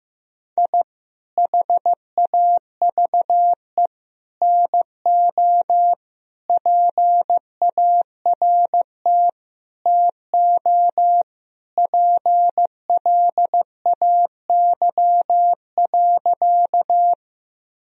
Morse code, heard as a keyed tone, about 15 wpm, 700 Hz. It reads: I HAVE NO PART TO PLAY.
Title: Sentences from Top 1000 Words - Review Part 1 - 15wpm